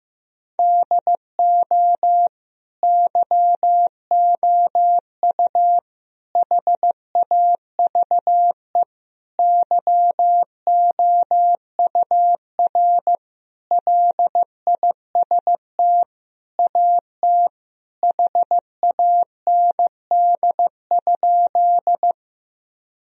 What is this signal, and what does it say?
Morse code, heard as a keyed tone, about 15 wpm, 700 Hz. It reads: DO YOU HAVE YOUR LIST AT HAND?
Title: Sentences from Top 1000 Words - Review Part 2 - 15wpm